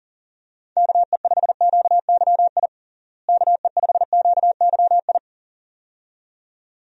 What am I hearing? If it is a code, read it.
Morse code, heard as a keyed tone, about 37 wpm, 700 Hz. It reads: KE5QYI KE5QYI
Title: USA Call Signs - 6 Characters - Repeated 2x - 40wpm